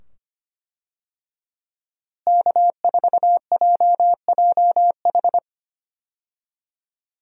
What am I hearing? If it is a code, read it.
Morse code, heard as a keyed tone, about 25 wpm, 700 Hz. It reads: K4JJH
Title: USA Call Signs - 5 Characters - Repeated 2x - 25wpm